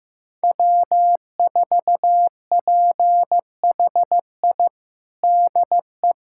Morse code, heard as a keyed tone, about 16 wpm, 700 Hz. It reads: W4PHI DE
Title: USA Callsign DE Callsign - 5 Characters - 15wpm